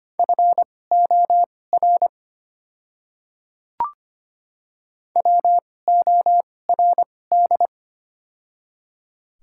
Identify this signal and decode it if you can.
Morse code, heard as a keyed tone, about 25 wpm, spaced as for 18 wpm, 700 Hz. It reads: FOR WORD
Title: 2x Character Spacing - Top 100 Words - 25wpm